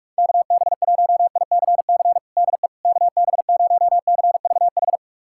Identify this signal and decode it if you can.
Morse code, heard as a keyed tone, about 46 wpm, 700 Hz. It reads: KB1ICC DE KB0CVH
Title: USA Callsign DE Callsign 45wpm